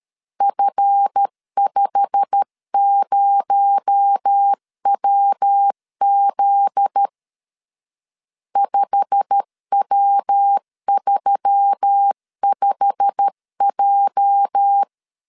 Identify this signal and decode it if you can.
Morse code, heard as a keyed tone, about 13 wpm, 800 Hz. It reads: F50WZ 5W35J
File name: LEZIONI / 8 YJ45 / 8 YJ45 12wpm 2151